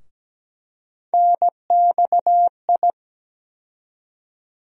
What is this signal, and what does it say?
Morse code, heard as a keyed tone, about 17 wpm, 700 Hz. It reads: NXI
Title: USA Call Sign Postfixes - 3 Letters 17wpm